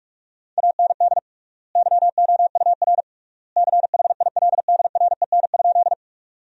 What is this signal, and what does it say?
Morse code, heard as a keyed tone, about 47 wpm, 700 Hz. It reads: AND YOUR CHILDREN?
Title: 3x Word-Spacing - Sentences from Top 300 Words - 45wpm